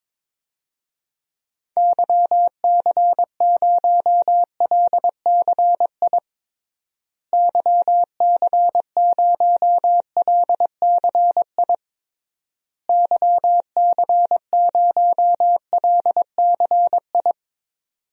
Morse code, heard as a keyed tone, about 22 wpm, 700 Hz. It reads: YC0LCI YC0LCI YC0LCI
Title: DXCC-100 Call Signs - 6 Characters - Repeated 3x - 22wpm